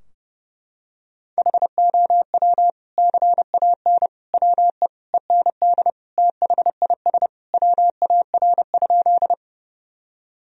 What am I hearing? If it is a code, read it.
Morse code, heard as a keyed tone, about 30 wpm, 700 Hz. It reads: HOW CAN WE END THIS WAR?
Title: Sentences from Top 1000 Words - Review Part 2 - 30wpm